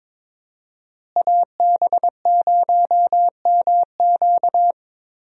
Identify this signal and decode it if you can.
Morse code, heard as a keyed tone, about 22 wpm, 700 Hz. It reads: AB0MQ
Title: USA Call Signs - 5 Characters - Repeated 2x - 22wpm